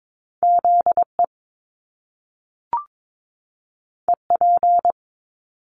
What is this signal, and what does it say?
Morse code, heard as a keyed tone, about 22 wpm, 700 Hz. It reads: ZE EP
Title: Top Two Letter Combinations in Words 22wpm